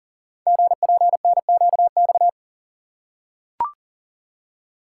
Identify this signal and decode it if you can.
Morse code, heard as a keyed tone, about 42 wpm, 700 Hz. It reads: GPNQX
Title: Code Groups - 5 Letter:Numbers - 40wpm